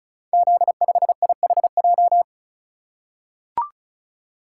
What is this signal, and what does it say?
Morse code, heard as a keyed tone, about 35 wpm, 690 Hz. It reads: Z5IHJ
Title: Code Groups - 5 Letter:Numbers - 35wpm